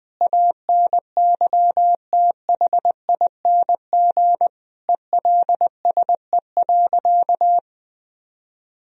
Morse code, heard as a keyed tone, about 21 wpm, 700 Hz. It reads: ANYTHING ELSE.
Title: Sentences from Top 1000 Words - Review Part 1 - 20wpm